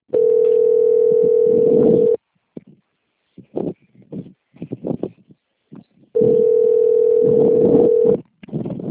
Ringback tone (North America), 440 + 480 Hz.